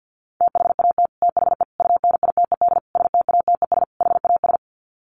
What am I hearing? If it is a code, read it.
Morse code, heard as a keyed tone, about 48 wpm, 700 Hz. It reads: THAT THE UNITED STATES HAS
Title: Sets of 5 Words Encore 50wpm